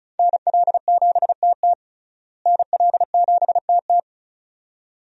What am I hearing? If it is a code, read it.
Morse code, heard as a keyed tone, about 36 wpm, 700 Hz. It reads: NL7TT NL7TT
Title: USA Call Signs - 5 Characters - Repeated 2x - 35wpm